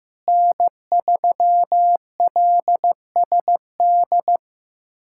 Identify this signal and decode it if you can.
Morse code, heard as a keyed tone, about 15 wpm, 700 Hz. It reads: N3LSD
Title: USA Callsign DE Callsign - 5 Characters - 15wpm